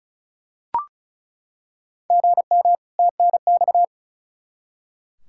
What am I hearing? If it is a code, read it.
Morse code, heard as a keyed tone, about 36 wpm, 700 Hz. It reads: GM TNX
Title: Practice QSO - Exchange 1 - Beginning - 35wpm